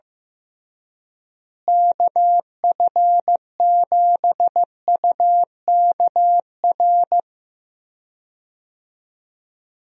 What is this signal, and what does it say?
Morse code, heard as a keyed tone, about 15 wpm, 700 Hz. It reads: KF7UKR
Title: USA Call Signs - 6 Characters - Repeated 2x - 15wpm